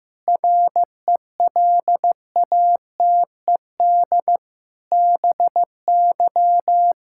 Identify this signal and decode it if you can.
Morse code, heard as a keyed tone, about 15 wpm, 700 Hz. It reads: RELATED BY
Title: Sentences from Top 1000 Words - Review Part 5 - 15wpm